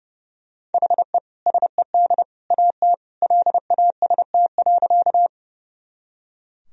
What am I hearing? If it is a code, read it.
Morse code, heard as a keyed tone, about 30 wpm, 700 Hz. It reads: HE SED AT LAST.
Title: Sentences from Top 300 Words and Abbreviations 30wpm